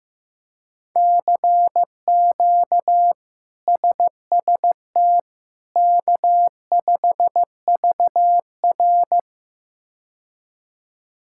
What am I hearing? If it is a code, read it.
Morse code, heard as a keyed tone, about 16 wpm, 700 Hz. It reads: CQ SST K5VR
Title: K1USN SST QSO Practice - 15wpm